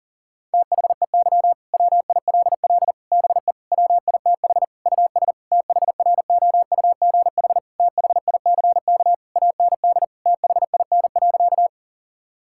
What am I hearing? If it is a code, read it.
Morse code, heard as a keyed tone, about 40 wpm, 700 Hz. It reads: THEY WILL BE WITH US THROUGH THICK AND THIN.
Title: Sentences from Top 1600 Words 40wpm